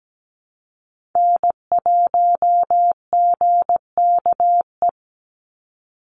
Morse code, heard as a keyed tone, about 17 wpm, 700 Hz. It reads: N1GKE
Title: Call Signs Repeated 3x - 17wpm